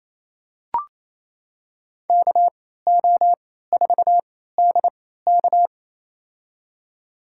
Morse code, heard as a keyed tone, about 27 wpm, spaced as for 16 wpm, 690 Hz. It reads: KO4DK
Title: USA Call Signs - 3x Character Spacing - 28wpm